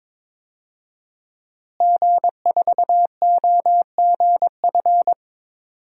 Morse code, heard as a keyed tone, about 22 wpm, 700 Hz. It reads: G4OGF
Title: DXCC-100 Call Signs - 5 Characters - Repeated 2x - 22wpm